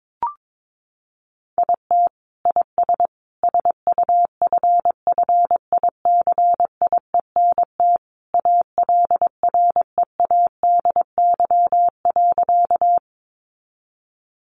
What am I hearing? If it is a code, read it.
Morse code, heard as a keyed tone, about 22 wpm, 700 Hz. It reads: IT IS SUFFICIENT ALREADY.
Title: Sentences from Top 1400 Words 22wpm